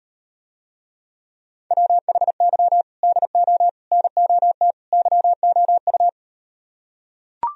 Morse code, heard as a keyed tone, about 38 wpm, 700 Hz. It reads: WHY DO NOT YOU